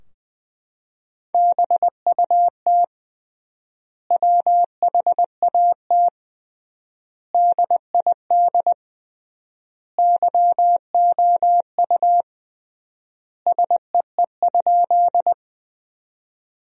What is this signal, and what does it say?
Morse code, heard as a keyed tone, about 21 wpm, 700 Hz. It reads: BUT WHAT DID YOU SEE?
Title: 3x Word-Spacing - Sentences from Top 100 Words - 20wpm